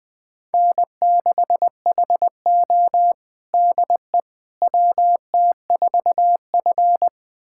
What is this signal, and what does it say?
Morse code, heard as a keyed tone, about 20 wpm, 700 Hz. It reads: N6HO DE WT4F
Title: USA Callsign DE Callsign - 4 Characters - 20wpm